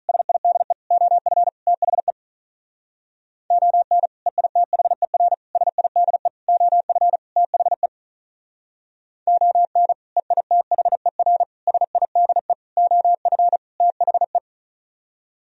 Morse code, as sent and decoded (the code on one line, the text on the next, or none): ... .. -.. / --- ..-. / - .... / --- -. / . .. - .... .-. / ... .. -.. / --- ..-. / - .... / --- -. / . .. - .... .-. / ... .. -.. / --- ..-. / - .... .
SIDE OF THE ON EITHER SIDE OF THE ON EITHER SIDE OF THE